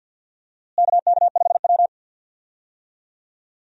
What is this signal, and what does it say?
Morse code, heard as a keyed tone, about 53 wpm, 690 Hz. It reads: KK5W
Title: USA Call Signs - 4 Characters - Repeated 2x - 50wpm